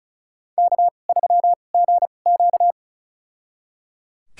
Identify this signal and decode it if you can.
Morse code, heard as a keyed tone, about 34 wpm, spaced as for 25 wpm, 700 Hz. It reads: K3GQ
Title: USA Call Signs - 4 Characters - 2x Character Spacing - 35wpm